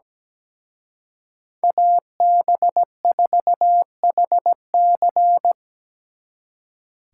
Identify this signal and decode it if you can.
Morse code, heard as a keyed tone, about 20 wpm, 700 Hz. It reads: AB4HC
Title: Speed Racing - USA 2x2 Call Signs 17wpm